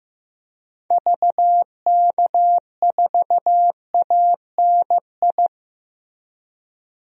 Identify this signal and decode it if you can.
Morse code, heard as a keyed tone, about 15 wpm, 700 Hz. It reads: VK4ANI
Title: DXCC-100 Call Signs - 6 Characters - Repeated 2x - 15wpm